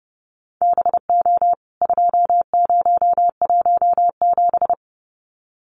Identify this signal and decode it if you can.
Morse code, heard as a keyed tone, about 30 wpm, 700 Hz. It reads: BO 2017
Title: CWT Exchange 30wpm